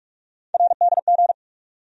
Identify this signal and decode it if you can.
Morse code, heard as a keyed tone, about 47 wpm, 700 Hz. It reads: RDG
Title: USA Call Sign Postfixes - 3 Letters 45wpm